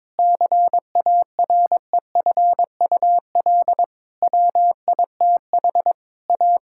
Morse code, heard as a keyed tone, about 22 wpm, 700 Hz. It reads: CAREFUL WITH A
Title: Sentences from Top 1600 Words 22wpm